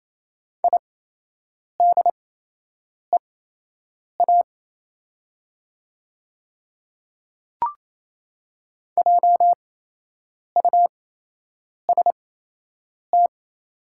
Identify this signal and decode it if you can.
Morse code, heard as a keyed tone, about 28 wpm, spaced as for 8 wpm, 690 Hz. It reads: IDEA JUST
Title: 8x Character Spacing - Top 500 Words - 28wpm